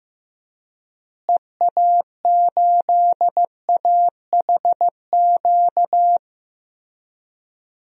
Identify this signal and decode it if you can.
Morse code, heard as a keyed tone, about 15 wpm, 700 Hz. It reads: EA8AHQ
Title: DXCC-100 Call Signs - 6 Characters - Repeated 3x - 15wpm